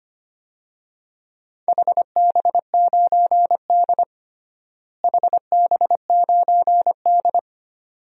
Morse code, heard as a keyed tone, about 25 wpm, 700 Hz. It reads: HB9D HB9D
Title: DXCC-100 Call Signs - 4 Characters - Repeated 2x - 25wpm